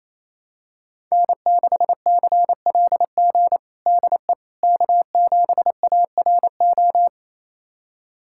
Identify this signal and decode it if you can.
Morse code, heard as a keyed tone, about 28 wpm, 700 Hz. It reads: N6CLG DE K7ARO
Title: USA Callsign DE Callsign - 5 Characters - 28wpm